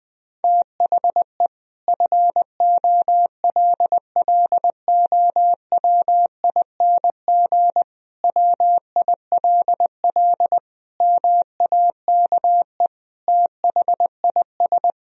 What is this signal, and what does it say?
Morse code, heard as a keyed tone, about 20 wpm, 700 Hz. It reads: THE FOLLOWING WILL MAKE THIS
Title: Sentences from Top 1000 Words - Review Part 4 - 20wpm